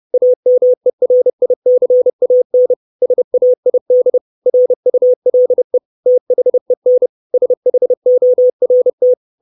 Morse code, heard as a keyed tone, about 30 wpm, 500 Hz. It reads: AMERICAN SAID RULE THEN SHORT
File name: 632 Words 30WPM